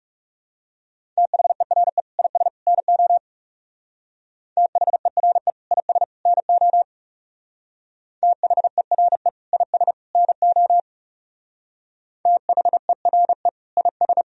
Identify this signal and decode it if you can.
Morse code, heard as a keyed tone, about 36 wpm, 700 Hz. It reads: THERE IS NO THERE IS NO THERE IS NO THERE IS
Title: Speed-Racing - Sets of 3 Words 30wpm